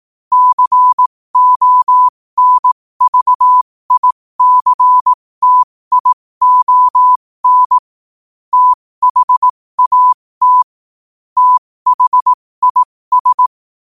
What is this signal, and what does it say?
Morse code, heard as a keyed tone, about 18 wpm, spaced as for 15 wpm, 1000 Hz. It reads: CONVICTION THAT THIS
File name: qotdfiles / 15wpm